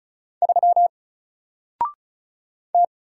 Morse code, heard as a keyed tone, about 34 wpm, 700 Hz. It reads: 3 T
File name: Single-Letter-Number-35wpm